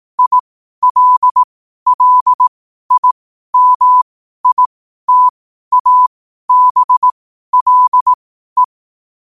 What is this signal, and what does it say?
Morse code, heard as a keyed tone, about 18 wpm, spaced as for 12 wpm, 1000 Hz. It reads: ILLIMITABLE
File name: qotdfiles / 12wpm